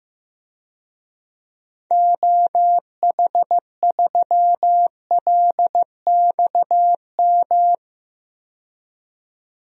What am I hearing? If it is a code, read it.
Morse code, heard as a keyed tone, about 15 wpm, 700 Hz. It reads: OH3LXM
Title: DXCC-100 Call Signs - 6 Characters - 15wpm